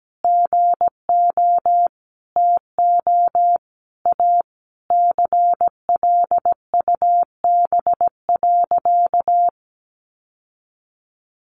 Morse code, heard as a keyed tone, about 17 wpm, 700 Hz. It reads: GO TO A CLUB.
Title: Sentences from Top 1200 Words 17wpm